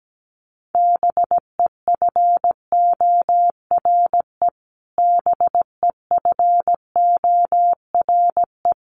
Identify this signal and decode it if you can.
Morse code, heard as a keyed tone, about 17 wpm, 700 Hz. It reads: BEFORE BEFORE
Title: Top 500 Words Repeated 17wpm